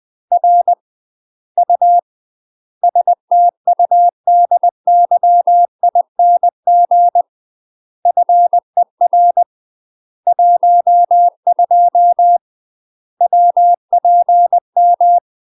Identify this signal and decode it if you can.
Morse code, heard as a keyed tone, about 20 wpm, 700 Hz. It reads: R U STUDYING FER 12 WPM
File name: UK MASTER QSOs TRAINING / 20WPM / UK MASTER QSOs 0011